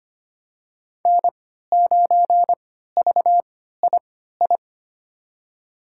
Morse code, heard as a keyed tone, about 25 wpm, spaced as for 14 wpm, 700 Hz. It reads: N9VII
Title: USA Call Signs - 5 Characters - 3x Character Spacing - 25wpm